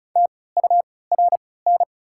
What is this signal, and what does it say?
Morse code, heard as a keyed tone, about 36 wpm, spaced as for 20 wpm, 690 Hz. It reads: TURN
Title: 3x Character Spacing - Top 300 Words - 35wpm